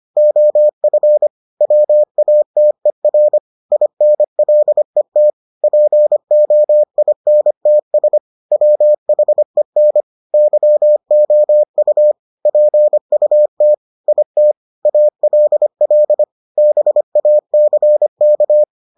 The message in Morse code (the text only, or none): OF WATER INLET POINTS WHEN YOU PUT IT ALL BACK